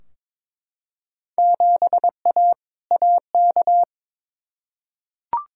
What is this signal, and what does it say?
Morse code, heard as a keyed tone, about 21 wpm, 700 Hz. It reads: 7A AK
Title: Field Day Exchanges Encore - 22wpm